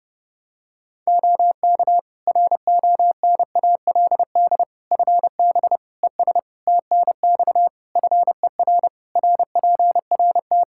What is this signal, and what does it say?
Morse code, heard as a keyed tone, about 30 wpm, 700 Hz. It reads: OK RONALD FB ES TNX FER RPRT
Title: Practice QSO - Exchange 2 - Info - 30wpm